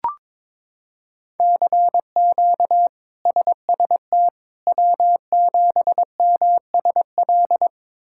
Morse code, heard as a keyed tone, about 21 wpm, 700 Hz. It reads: CQ SST W7MSL
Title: K1USN SST QSO Practice - 22wpm